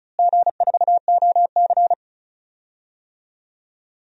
Morse code, heard as a keyed tone, about 36 wpm, 700 Hz. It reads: G4OC
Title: DXCC-100 Call Signs - 4 Characters - Repeated 3x - 35wpm